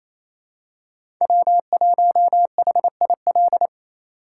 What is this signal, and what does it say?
Morse code, heard as a keyed tone, about 28 wpm, 700 Hz. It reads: W1HIL
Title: USA Call Signs - 5 Characters - 2x Character Spacing - 28wpm